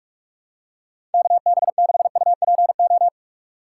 Morse code, heard as a keyed tone, about 46 wpm, 700 Hz. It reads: KB6UPO
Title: USA Call Signs - 6 Characters - 3x Character Spacing - 45wpm